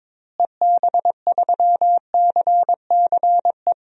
Morse code, heard as a keyed tone, about 22 wpm, 700 Hz. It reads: EB3CCE
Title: DXCC-100 Call Signs - 6 Characters - Repeated 3x - 22wpm